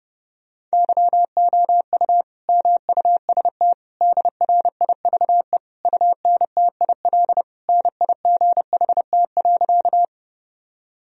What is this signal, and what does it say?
Morse code, heard as a keyed tone, about 32 wpm, 700 Hz. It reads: YOU MUST DRIVE UNTIL NIGHT.
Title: Sentences from Top 1000 Words - Review Part 2 - 30wpm